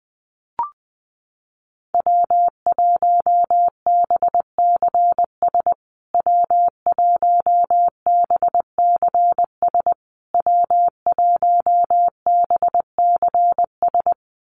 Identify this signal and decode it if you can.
Morse code, heard as a keyed tone, about 20 wpm, 700 Hz. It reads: W1BCS W1BCS W1BCS
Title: USA Call Signs Repeated 3x 20wpm